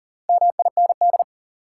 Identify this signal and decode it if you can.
Morse code, heard as a keyed tone, about 40 wpm, 700 Hz. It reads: MIND
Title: Sets of 5 Words - New Word Drill - 40wpm